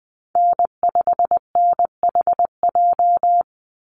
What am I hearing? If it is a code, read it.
Morse code, heard as a keyed tone, about 20 wpm, 700 Hz. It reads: N5NHJ
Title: Call Signs - 20wpm